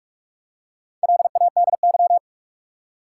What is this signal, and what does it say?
Morse code, heard as a keyed tone, about 46 wpm, 700 Hz. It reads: LADY
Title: Four Letter Words - 45wpm